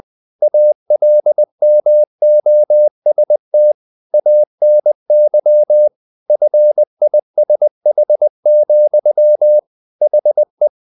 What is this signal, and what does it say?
Morse code, heard as a keyed tone, about 20 wpm, 600 Hz. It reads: ALMOST ANY FISH, HE
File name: the old man and the sea part 2 20wpm